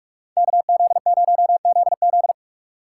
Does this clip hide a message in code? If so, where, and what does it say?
Yes: Morse code from 0.4 to 2.3 s, heard as a keyed tone, about 59 wpm, 690 Hz. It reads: KZ0ZZ